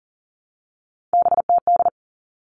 Morse code, heard as a keyed tone, about 40 wpm, 700 Hz. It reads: BTD
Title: All Cut Numbers - 3-Digit 40wpm